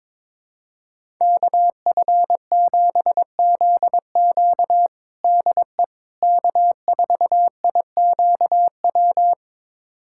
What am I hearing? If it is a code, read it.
Morse code, heard as a keyed tone, about 21 wpm, 700 Hz. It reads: KF7ZQ DE K4IQW
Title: USA Callsign DE Callsign - 5 Characters - 22wpm